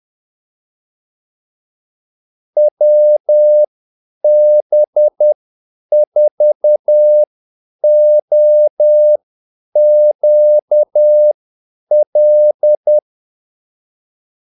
Morse code, heard as a keyed tone, about 10 wpm, spaced as for 8 wpm, 600 Hz. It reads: WB4OQL
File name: Callsigns-10-8wpm